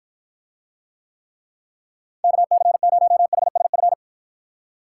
Morse code, heard as a keyed tone, about 53 wpm, 700 Hz. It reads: KC0HSF